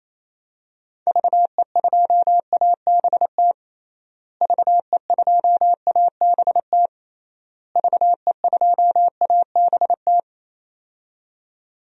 Morse code, heard as a keyed tone, about 28 wpm, 700 Hz. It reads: VE2ABT VE2ABT VE2ABT
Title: DXCC-100 Call Signs - 6 Characters - Repeated 3x - 28wpm